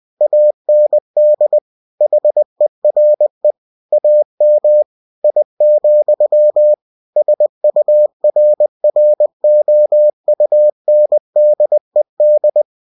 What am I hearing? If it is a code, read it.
Morse code, heard as a keyed tone, about 20 wpm, 600 Hz. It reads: AND HERE AM I, SURROUNDED
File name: cw-books / winnie-the-pooh / winnie-20wpm-page-0091